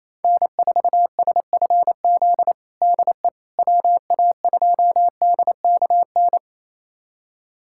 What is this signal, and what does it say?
Morse code, heard as a keyed tone, about 28 wpm, 700 Hz. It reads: N4SFZ DE WA2DKN
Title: USA Callsign DE Callsign 28wpm